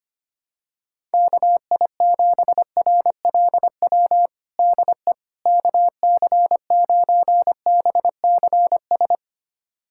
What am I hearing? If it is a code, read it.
Morse code, heard as a keyed tone, about 25 wpm, 700 Hz. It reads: KI7RLW DE KC9BCS
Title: USA Callsign DE Callsign - 6 Characters - 25wpm